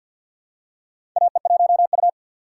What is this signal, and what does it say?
Morse code, heard as a keyed tone, about 51 wpm, 700 Hz. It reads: AE1U
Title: USA Call Signs - 3x Character Spacing - 50wpm